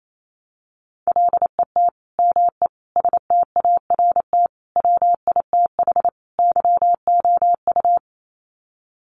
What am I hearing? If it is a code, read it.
Morse code, heard as a keyed tone, about 28 wpm, 700 Hz. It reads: LET ME START WITH YOU
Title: Sets of 5 Words Encore 28wpm